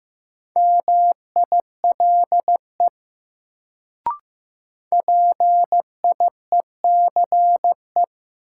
Morse code, heard as a keyed tone, about 15 wpm, 700 Hz. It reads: MILE PIECE
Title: Sentences from Top 400 Words - New Word Drill - 15wpm